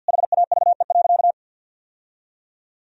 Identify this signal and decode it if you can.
Morse code, heard as a keyed tone, about 47 wpm, 700 Hz. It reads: HAVE.
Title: Sentences from Top 1000 Words - Review Part 3 - 50wpm